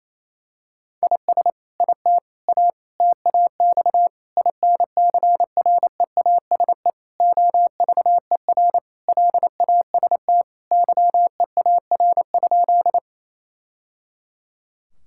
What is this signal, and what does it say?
Morse code, heard as a keyed tone, about 28 wpm, 700 Hz. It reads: IS IT A TAX INCREASE OVER LAST YEAR?
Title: Sentences from Top 1000 Words - Review Part 5 - 28wpm